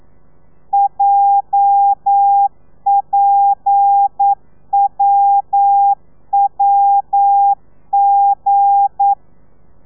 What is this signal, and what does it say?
Morse code, heard as a keyed tone, about 9 wpm, 800 Hz. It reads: JPWWG
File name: telegrafia / i2tfj / Lez8 40